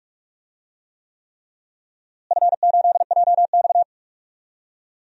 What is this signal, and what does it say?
Morse code, heard as a keyed tone, about 41 wpm, 700 Hz. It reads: F8JX
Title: DXCC-100 Call Signs - 4 Characters - Repeated 3x - 45wpm